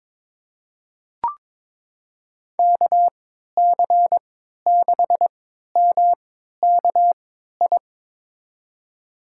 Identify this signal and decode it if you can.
Morse code, heard as a keyed tone, about 22 wpm, spaced as for 12 wpm, 700 Hz. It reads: KC6MKI
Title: USA Call Signs - 6 Characters - 3x Character Spacing - 22wpm